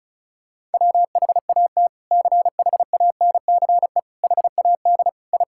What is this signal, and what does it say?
Morse code, heard as a keyed tone, about 34 wpm, 700 Hz. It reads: WHAT CHANCE HAD I
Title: Sentences from Top 1000 Words - Review Part 6 - 35wpm